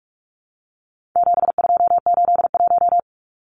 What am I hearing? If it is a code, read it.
Morse code, heard as a keyed tone, about 44 wpm, 700 Hz. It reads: 7281
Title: Numbers - 4-Digit 45wpm